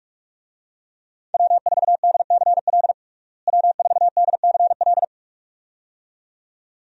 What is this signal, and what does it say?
Morse code, heard as a keyed tone, about 46 wpm, 700 Hz. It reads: W4DCL W4DCL
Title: USA Call Signs - 5 Characters - Repeated 2x - 45wpm